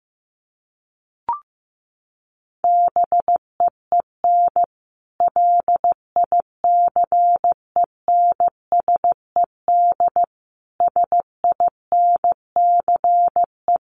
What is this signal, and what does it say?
Morse code, heard as a keyed tone, about 15 wpm, 700 Hz. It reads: BEEN LICENSED SINCE
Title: Common QSO Phrases 15wpm